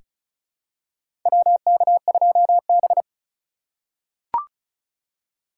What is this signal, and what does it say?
Morse code, heard as a keyed tone, about 36 wpm, 700 Hz. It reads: WK2B
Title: USA Call Signs - 8x Character Spacing - 35wpm